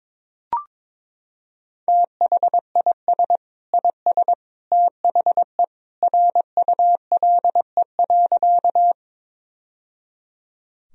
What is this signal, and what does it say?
Morse code, heard as a keyed tone, about 22 wpm, 700 Hz. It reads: THIS IS THE RULE.
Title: Sentences from Top 1000 Words - Review Part 2 - 22wpm